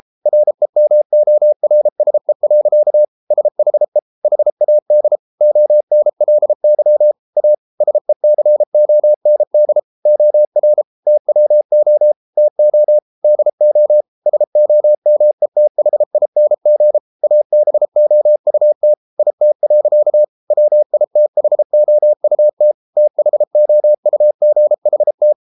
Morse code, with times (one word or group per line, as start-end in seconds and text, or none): 0.3-3.0 REMORSE.
3.3-4.0 SHE
4.2-5.2 HAD
5.4-7.1 ONLY
7.4-7.5 A
7.8-9.8 SECOND
10.1-10.8 OR
11.1-12.1 TWO
12.4-13.0 TO
13.2-14.0 DO
14.3-17.0 SOMETHING
17.2-18.9 ABOUT
19.2-20.2 IT.
20.5-22.7 WITHOUT
23.0-25.3 THOUGHT